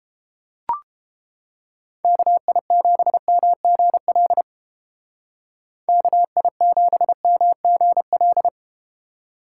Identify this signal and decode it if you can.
Morse code, heard as a keyed tone, about 34 wpm, 700 Hz. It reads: KI7MGL KI7MGL